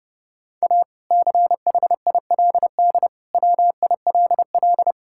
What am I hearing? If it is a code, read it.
Morse code, heard as a keyed tone, about 32 wpm, 700 Hz. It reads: A CHILD WILL
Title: Sentences from Top 1000 Words - Review Part 6 - 30wpm